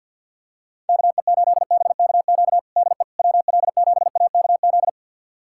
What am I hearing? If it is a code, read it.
Morse code, heard as a keyed tone, about 50 wpm, 690 Hz. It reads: KE8BKQ DE WL7AKZ
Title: USA Callsign DE Callsign - 6 Characters - 50wpm